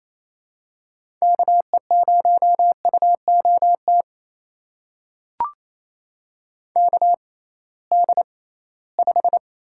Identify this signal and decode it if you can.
Morse code, heard as a keyed tone, about 28 wpm, spaced as for 18 wpm, 700 Hz. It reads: KE0UOT KD5
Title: USA Call Signs - 6 Characters - 6x Character Spacing - 28wpm